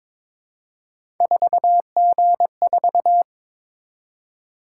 Morse code, heard as a keyed tone, about 22 wpm, 700 Hz. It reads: 4G4
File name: practice / 22wpm / Three Letter-Number Combinations - 22wpm